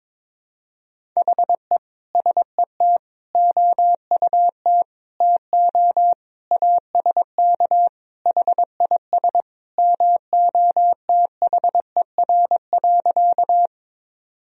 Morse code, heard as a keyed tone, about 22 wpm, 700 Hz. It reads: HE SET OUT TO ASK HIS MOTHER.
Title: Sentences from Top 1000 Words - Review Part 1 - 22wpm